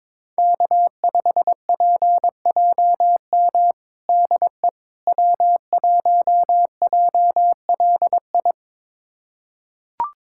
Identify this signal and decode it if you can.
Morse code, heard as a keyed tone, about 22 wpm, 700 Hz. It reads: K5PJM DE W1JLI
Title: USA Callsign DE Callsign - 5 Characters - 22wpm